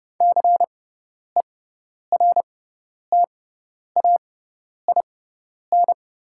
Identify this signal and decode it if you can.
Morse code, heard as a keyed tone, about 30 wpm, spaced as for 10 wpm, 690 Hz. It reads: CERTAIN
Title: 6x Character Spacing - Top 500 Words - 30wpm